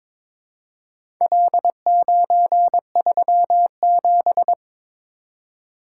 Morse code, heard as a keyed tone, about 22 wpm, 700 Hz. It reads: L937
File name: practice / 22wpm / Four Letter-Number Combinations - 22wpm